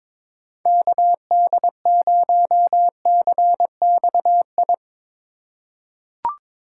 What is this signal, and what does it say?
Morse code, heard as a keyed tone, about 21 wpm, 700 Hz. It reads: KD0CXI
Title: USA Call Signs - 4x Character Spacing - 22wpm